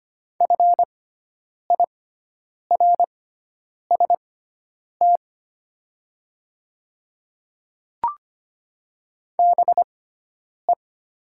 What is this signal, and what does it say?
Morse code, heard as a keyed tone, about 25 wpm, spaced as for 9 wpm, 700 Hz. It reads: FIRST BE
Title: 6x Character Spacing - Top 100 Words - 25wpm